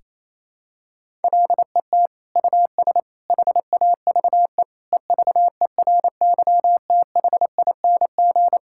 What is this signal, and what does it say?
Morse code, heard as a keyed tone, about 28 wpm, 700 Hz. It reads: LET US HAVE EVERYTHING
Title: Sentences from Top 1000 Words - Review Part 6 - 28wpm